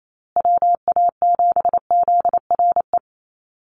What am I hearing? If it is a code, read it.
Morse code, heard as a keyed tone, about 28 wpm, 700 Hz. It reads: WA7ZRE
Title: USA Call Signs Repeated 2x 28wpm